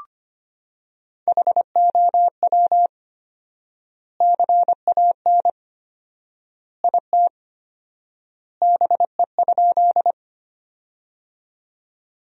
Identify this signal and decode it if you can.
Morse code, heard as a keyed tone, about 25 wpm, 700 Hz. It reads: HOW CAN IT BE?
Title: 4x Word-Spacing - Sentences from Top 100 Words - 25wpm